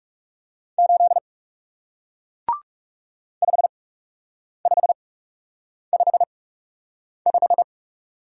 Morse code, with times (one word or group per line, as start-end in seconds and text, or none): 0.8-1.2 8
3.4-7.6 5555